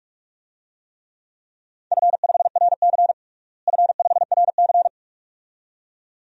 Morse code, heard as a keyed tone, about 44 wpm, 700 Hz. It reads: F5RC F5RC
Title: DXCC-100 Call Signs - 4 Characters - Repeated 2x - 45wpm